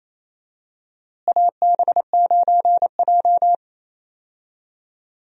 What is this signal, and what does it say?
Morse code, heard as a keyed tone, about 28 wpm, 700 Hz. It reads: AB9J
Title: USA Call Signs - 4 Characters - Repeated 2x - 28wpm